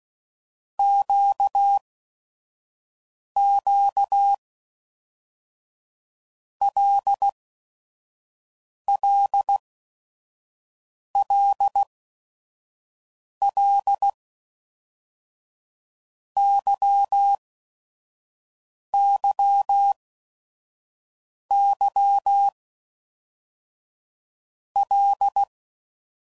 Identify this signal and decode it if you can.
Morse code, heard as a keyed tone, about 16 wpm, 800 Hz. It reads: Q Q L L L L Y Y Y L
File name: Basic Letter Groups By7Groups / LFCYQ / LFCYQ@5 f15